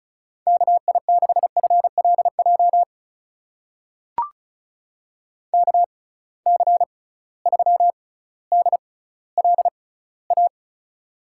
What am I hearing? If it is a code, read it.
Morse code, heard as a keyed tone, about 36 wpm, spaced as for 18 wpm, 700 Hz. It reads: KI6FLJ KC3DLA
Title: USA Call Signs - 6 Characters - 6x Character Spacing - 35wpm